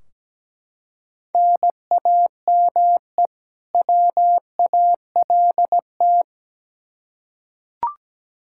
Morse code, heard as a keyed tone, about 17 wpm, 700 Hz. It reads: NAME WALT
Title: Practice QSO - Exchange 1 - Beginning - 17wpm